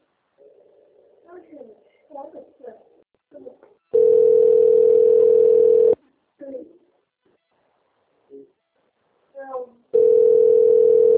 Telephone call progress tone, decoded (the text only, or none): ringback tone